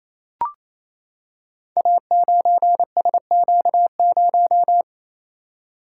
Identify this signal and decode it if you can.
Morse code, heard as a keyed tone, about 27 wpm, 700 Hz. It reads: A9SQ0
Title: Code Groups - 5 Letter:Numbers - 28wpm